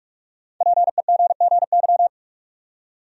Morse code, heard as a keyed tone, about 46 wpm, 700 Hz. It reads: PEGGY
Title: USA Common Names - 45wpm